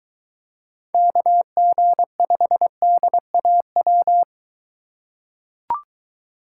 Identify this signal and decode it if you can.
Morse code, heard as a keyed tone, about 23 wpm, 700 Hz. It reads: KG5DAW